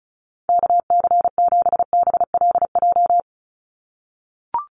Morse code, heard as a keyed tone, about 34 wpm, 700 Hz. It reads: KC7BLJ
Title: USA Call Signs 35wpm